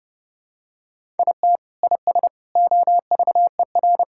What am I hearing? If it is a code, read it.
Morse code, heard as a keyed tone, about 30 wpm, 700 Hz. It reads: IT IS OVER
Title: Sentences from Top 1000 Words - Review Part 6 - 30wpm